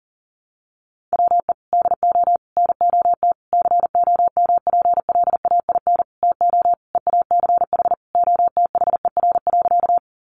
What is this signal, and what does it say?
Morse code, heard as a keyed tone, about 40 wpm, 700 Hz. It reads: WE DO NOT COMPLAIN TO EACH OTHER.
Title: Sentences from Top 1300 Words 40wpm